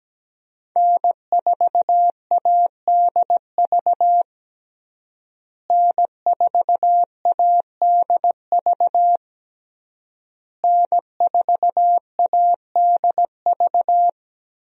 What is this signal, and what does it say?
Morse code, heard as a keyed tone, about 17 wpm, 700 Hz. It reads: N4ADV N4ADV N4ADV
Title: USA Call Signs - 5 Characters - Repeated 3x - 17wpm